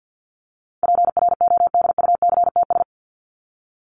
Morse code, heard as a keyed tone, about 50 wpm, 700 Hz. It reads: PRODUCTS
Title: ES:S Suffix - Top 100 Words with Base Words 50wpm